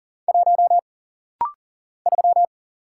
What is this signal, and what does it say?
Morse code, heard as a keyed tone, about 40 wpm, 700 Hz. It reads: 1 3
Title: Single Letter-Number - Rapid-Fire 40wpm